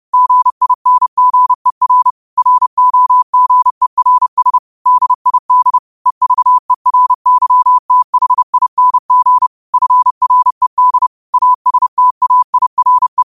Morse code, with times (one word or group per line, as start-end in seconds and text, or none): 0.1-2.1 GINGER
2.4-4.6 ROGERS
4.9-5.8 DID
6.1-9.5 EVERYTHING
9.7-11.1 FRED
11.3-13.2 ASTAIRE